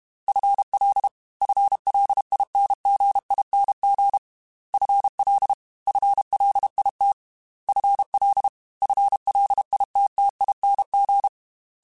Morse code, heard as a keyed tone, about 32 wpm, 800 Hz. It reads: FL FLINGING FL FLIT FL FLITTING